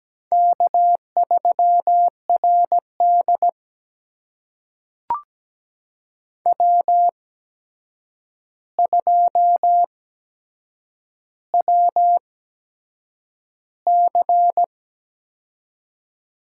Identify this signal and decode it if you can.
Morse code, heard as a keyed tone, about 17 wpm, spaced as for 7 wpm, 700 Hz. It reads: K3RD W2WC
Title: USA Call Signs - 4 Characters - 8x Character Spacing - 17wpm